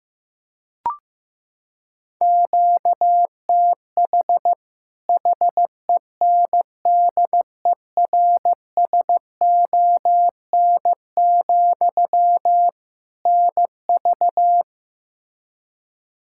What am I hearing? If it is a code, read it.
Morse code, heard as a keyed tone, about 15 wpm, 700 Hz. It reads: QTH HENDERSON, NV
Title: Practice QSO - Exchange 1 - Beginning - 15wpm